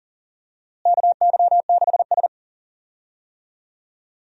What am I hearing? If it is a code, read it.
Morse code, heard as a keyed tone, about 40 wpm, 700 Hz. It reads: KY6S
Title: USA Call Signs - 4 Characters - Repeated 3x - 40wpm